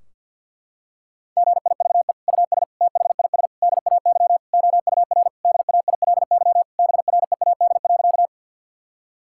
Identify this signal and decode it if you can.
Morse code, heard as a keyed tone, about 51 wpm, 700 Hz. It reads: GIVE US THIS DAY OUR DAILY BREAD.
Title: Sentences from Top 1600 Words 50wpm